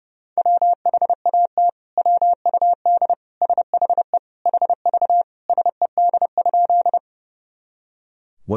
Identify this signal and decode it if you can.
Morse code, heard as a keyed tone, about 30 wpm, 700 Hz. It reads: WHAT WUD SHE HV SED?
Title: Sentences from Top 100 Words and Abbreviations 30wpm